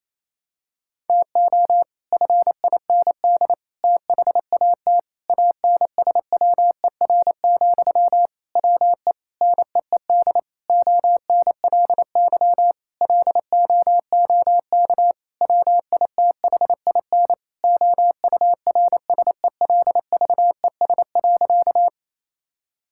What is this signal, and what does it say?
Morse code, heard as a keyed tone, about 28 wpm, 700 Hz. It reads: TO FIND THAT ANSWER, WE NEED ONLY LOOK WITHIN OURSELVES.
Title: Sentences from Top 1600 Words 28wpm